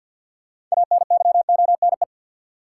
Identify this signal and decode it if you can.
Morse code, heard as a keyed tone, about 49 wpm, 700 Hz. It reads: ANYONE
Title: Sets of 5 Words Encore - New Word Drill - 50wpm